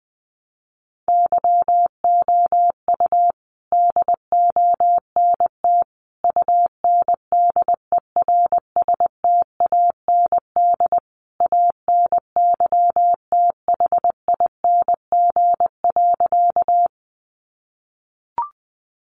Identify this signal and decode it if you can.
Morse code, heard as a keyed tone, about 20 wpm, 700 Hz. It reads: YOU DONT UNDERSTAND ANYTHING.
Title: Sentences from Top 1000 Words 20wpm